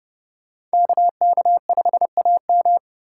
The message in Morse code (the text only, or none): KK5AM